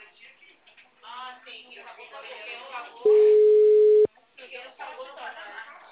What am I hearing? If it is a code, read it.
Ringback tone (Europe, 425 Hz), with speech.